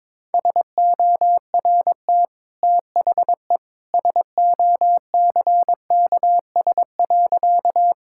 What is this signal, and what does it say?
Morse code, heard as a keyed tone, about 22 wpm, 700 Hz. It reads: SORT THE SOCKS.
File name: practice / 22wpm / Easy Sentences with Hints - 22wpm